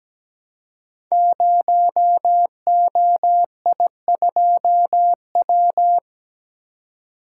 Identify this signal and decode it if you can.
Morse code, heard as a keyed tone, about 17 wpm, 700 Hz. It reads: 0OI2W
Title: Code Groups - 5 Letter:Numbers - 17wpm